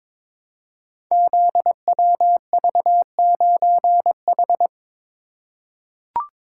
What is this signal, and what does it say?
Morse code, heard as a keyed tone, about 22 wpm, 700 Hz. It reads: ZWV9H
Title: Code Groups - 5 Letter:Numbers - 22wpm